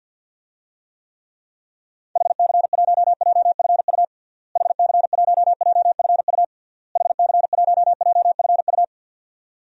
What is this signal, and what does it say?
Morse code, heard as a keyed tone, about 50 wpm, 700 Hz. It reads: HC1JFU HC1JFU HC1JFU
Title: DXCC-100 Call Signs - 6 Characters - Repeated 3x - 50wpm